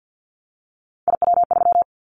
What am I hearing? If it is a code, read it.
Morse code, heard as a keyed tone, about 50 wpm, 700 Hz. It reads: IW3
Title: Callsign Prefixes for Top 25 DXCC - 50wpm